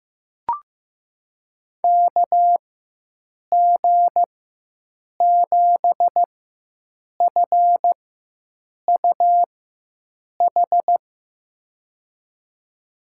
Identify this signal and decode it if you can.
Morse code, heard as a keyed tone, about 15 wpm, spaced as for 7 wpm, 700 Hz. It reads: KG7FUH
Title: USA Call Signs - 6 Characters - 4x Character Spacing - 15wpm